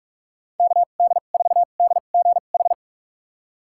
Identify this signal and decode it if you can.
Morse code, heard as a keyed tone, about 45 wpm, spaced as for 32 wpm, 700 Hz. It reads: KD4DGH
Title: USA Call Signs - 6 Characters - 2x Character Spacing - 45wpm